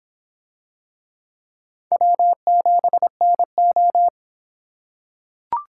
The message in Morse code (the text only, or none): W7NO